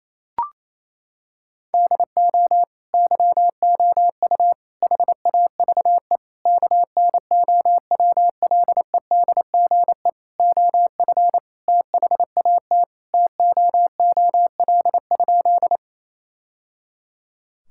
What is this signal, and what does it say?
Morse code, heard as a keyed tone, about 27 wpm, 700 Hz. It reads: DO YOU HAVE KNOWLEDGE OF THAT TOOL?
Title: Sentences from Top 1000 Words - Review Part 6 - 28wpm